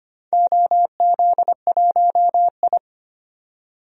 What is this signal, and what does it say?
Morse code, heard as a keyed tone, about 25 wpm, 700 Hz. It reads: OZ1I
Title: DXCC-100 Call Signs - 4 Characters - 25wpm